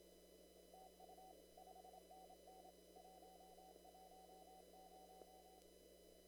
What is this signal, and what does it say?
Morse code, heard as a keyed tone, about 25 wpm, 750 Hz. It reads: TU 5NN 110
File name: CW / kt5j wpx cw-op-k5pi